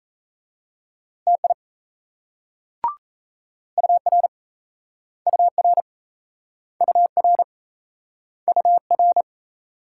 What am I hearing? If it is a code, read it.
Morse code, heard as a keyed tone, about 38 wpm, 700 Hz. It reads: TI UR UR UR UR